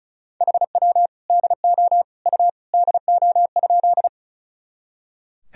Morse code, heard as a keyed tone, about 36 wpm, 700 Hz. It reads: HW DO U DO?
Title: Sentences from Top 100 Words and Abbreviations 35wpm